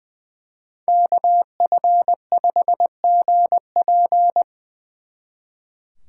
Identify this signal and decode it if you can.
Morse code, heard as a keyed tone, about 23 wpm, 700 Hz. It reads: KF5GP
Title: Speed Racing - USA Call Signs 20wpm